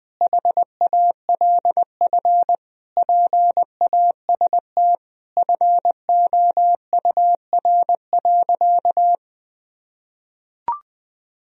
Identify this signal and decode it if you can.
Morse code, heard as a keyed tone, about 20 wpm, 700 Hz. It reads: HALF PAST FOUR.
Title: Sentences from Top 1000 Words - Review Part 3 - 20wpm